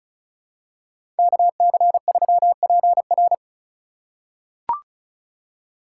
Morse code, heard as a keyed tone, about 36 wpm, 700 Hz. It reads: KC3PR
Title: USA Call Signs - 5 Characters - 4x Character Spacing - 35wpm